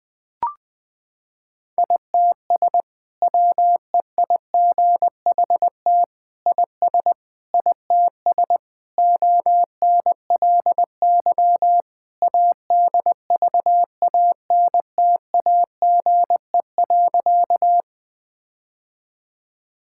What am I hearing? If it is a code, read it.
Morse code, heard as a keyed tone, about 20 wpm, 700 Hz. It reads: ITS WEIGHT IS ITS ONLY ADVANTAGE.
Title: Sentences from Top 1000 Words - Review Part 6 - 20wpm